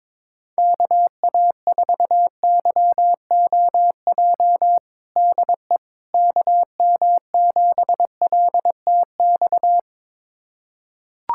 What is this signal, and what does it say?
Morse code, heard as a keyed tone, about 22 wpm, 700 Hz. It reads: KA4YOJ DE KM7LTX
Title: USA Callsign DE Callsign 22wpm